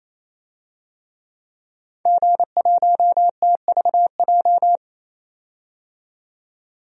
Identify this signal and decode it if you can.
Morse code, heard as a keyed tone, about 27 wpm, 700 Hz. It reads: G1TVJ
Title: DXCC-100 Call Signs - 5 Characters - 28wpm